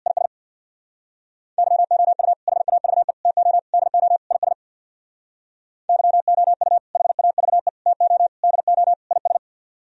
Morse code, heard as a keyed tone, about 64 wpm, 700 Hz. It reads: IS YOU HAVE TO DO IS YOU HAVE TO DO IS